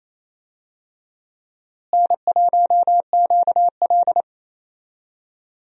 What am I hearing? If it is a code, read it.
Morse code, heard as a keyed tone, about 28 wpm, 700 Hz. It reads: N1QL
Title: USA Call Signs - 4 Characters - Repeated 2x - 28wpm